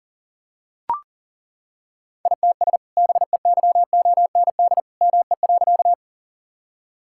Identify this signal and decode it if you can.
Morse code, heard as a keyed tone, about 40 wpm, 700 Hz. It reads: ITS BEYOND ME.
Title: Sentences from Top 1000 Words - Review Part 6 - 40wpm